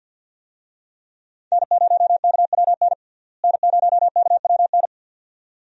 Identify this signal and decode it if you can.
Morse code, heard as a keyed tone, about 50 wpm, 700 Hz. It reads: N0KWN N0KWN
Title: USA Call Signs - 5 Characters - Repeated 2x - 50wpm